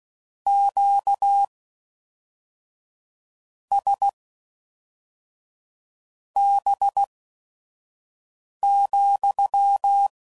Morse code, heard as a keyed tone, about 16 wpm, spaced as for 4 wpm, 800 Hz. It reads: QSB,